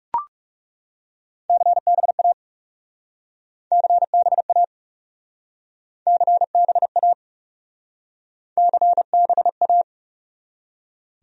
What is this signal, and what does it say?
Morse code, heard as a keyed tone, about 41 wpm, 700 Hz. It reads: CBA CBA CBA CBA